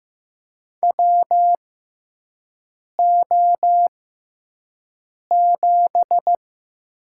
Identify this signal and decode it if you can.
Morse code, heard as a keyed tone, about 15 wpm, spaced as for 5 wpm, 700 Hz. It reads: WO7
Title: USA Call Signs - 6x Character Spacing - 15wpm